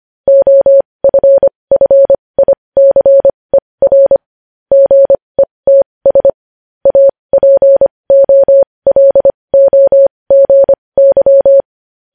Morse code, heard as a keyed tone, about 25 wpm, spaced as for 20 wpm, 550 Hz. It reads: OFFICER GETS APOLOGY
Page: News Headlines in Morse Code at 20 WPM